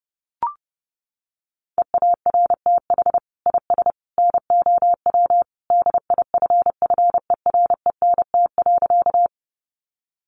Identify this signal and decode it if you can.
Morse code, heard as a keyed tone, about 30 wpm, 700 Hz. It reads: EARTH IS NOW DIFFERENT.
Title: Sentences from Top 600 Words 30wpm